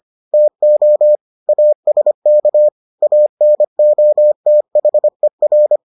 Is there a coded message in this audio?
Yes: Morse code from 0.3 to 5.8 s, heard as a keyed tone, about 25 wpm, 600 Hz. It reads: TO ASK ANOTHER